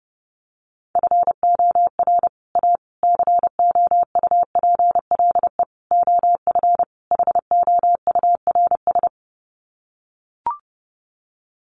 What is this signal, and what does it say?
Morse code, heard as a keyed tone, about 30 wpm, 700 Hz. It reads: FOR A COUPLE OF HOURS
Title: Sets of 5 Words Encore2 30wpm